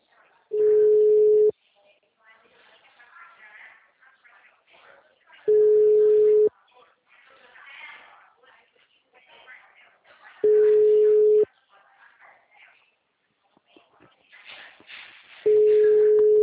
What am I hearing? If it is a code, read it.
Ringback tone (Europe, 425 Hz), with speech.